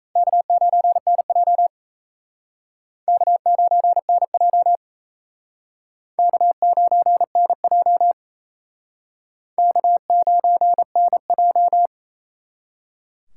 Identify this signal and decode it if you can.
Morse code, heard as a keyed tone, about 38 wpm, 700 Hz. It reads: K9NJ K9NJ K9NJ K9NJ